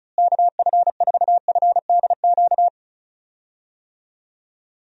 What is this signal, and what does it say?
Morse code, heard as a keyed tone, about 36 wpm, 700 Hz. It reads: KF4FDQ
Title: USA Call Signs - 6 Characters - Repeated 3x - 35wpm